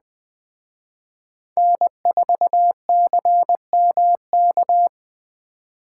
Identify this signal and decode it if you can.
Morse code, heard as a keyed tone, about 20 wpm, 700 Hz. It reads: N4CMK
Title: Speed Racing - USA Call Signs 17wpm